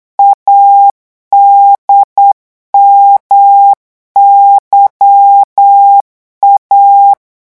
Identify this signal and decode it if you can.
Morse code, heard as a keyed tone, about 8 wpm, 800 Hz. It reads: ADMYA
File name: morse mp3 / revision3-vitesse 8 mots